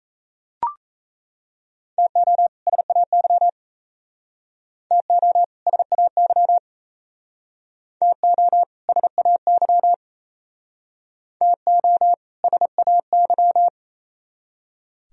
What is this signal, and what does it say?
Morse code, heard as a keyed tone, about 37 wpm, 700 Hz. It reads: TO SAY TO SAY TO SAY TO SAY